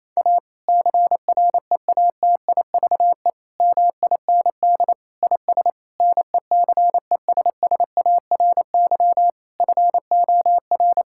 Morse code, heard as a keyed tone, about 28 wpm, 700 Hz. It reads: A CREATIVE MIND IS NECESSARY FOR
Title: Sentences from Top 1000 Words - Review Part 5 - 28wpm